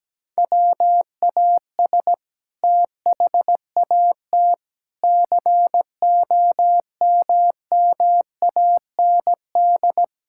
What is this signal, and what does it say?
Morse code, heard as a keyed tone, about 17 wpm, 700 Hz. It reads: WAS THAT COMMAND
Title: Sentences from Top 1600 Words 17wpm